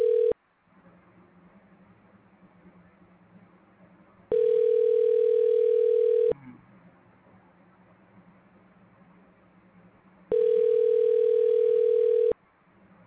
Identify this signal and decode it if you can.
Ringback tone (North America), 440 + 480 Hz.